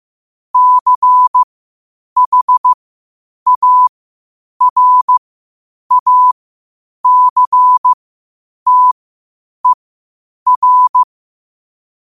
Morse code, heard as a keyed tone, about 15 wpm, spaced as for 8 wpm, 1000 Hz. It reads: CHARACTER
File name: qotdfiles / 8wpm